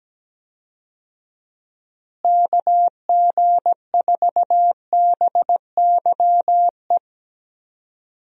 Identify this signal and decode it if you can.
Morse code, heard as a keyed tone, about 17 wpm, 700 Hz. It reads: KG4BYE
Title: USA Call Signs - 6 Characters - Repeated 3x - 17wpm